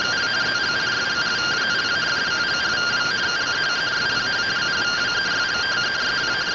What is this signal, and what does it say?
Two-tone teleprinter signal, about 170 Hz shift.